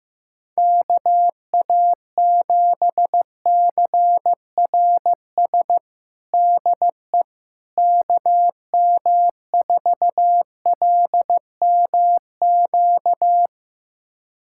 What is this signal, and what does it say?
Morse code, heard as a keyed tone, about 15 wpm, 700 Hz. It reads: KA7CRS DE KM4LMQ
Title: USA Callsign DE Callsign 15wpm